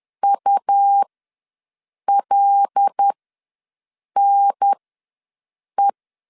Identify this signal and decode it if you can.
Morse code, heard as a keyed tone, about 10 wpm, spaced as for 6 wpm, 800 Hz. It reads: ULNE
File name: LEZIONI / 5 LBVC / 5 L ws 10wpm 3636